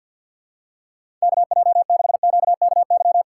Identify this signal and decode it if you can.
Morse code, heard as a keyed tone, about 53 wpm, 700 Hz. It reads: K J 6 Q K Y
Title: USA Call Signs - 6 Characters - 8x Character Spacing - 50wpm